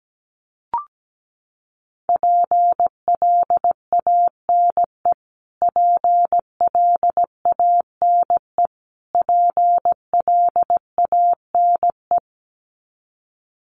Morse code, heard as a keyed tone, about 17 wpm, 700 Hz. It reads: PLANE PLANE PLANE
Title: Top 500 Words Repeated 17wpm